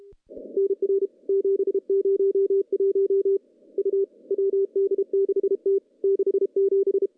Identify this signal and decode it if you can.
Morse code, heard as a keyed tone, about 31 wpm, 410 Hz. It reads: NR 701 U WD6T 67
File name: contests / CW-SS-k5tr- -0700